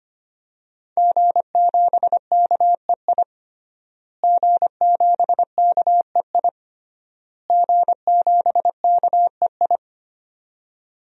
Morse code, heard as a keyed tone, about 25 wpm, 700 Hz. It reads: G7KEI G7KEI G7KEI
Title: DXCC-100 Call Signs - 5 Characters - Repeated 3x - 25wpm